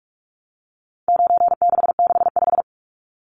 Morse code, heard as a keyed tone, about 44 wpm, 700 Hz. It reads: 9665
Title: Numbers - 4-Digit 45wpm